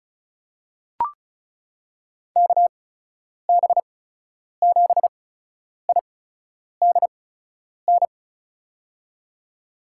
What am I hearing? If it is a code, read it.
Morse code, heard as a keyed tone, about 34 wpm, spaced as for 10 wpm, 700 Hz. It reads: KB7IDN
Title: USA Call Signs - 6 Characters - 8x Character Spacing - 35wpm